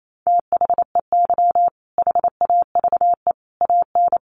Morse code, heard as a keyed tone, about 28 wpm, 700 Hz. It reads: THEY HAVE AN
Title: Sentences from Top 1000 Words 28wpm